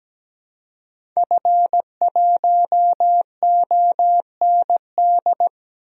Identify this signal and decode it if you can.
Morse code, heard as a keyed tone, about 17 wpm, 700 Hz. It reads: F1OND
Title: DXCC-100 Call Signs - 5 Characters - Repeated 2x - 17wpm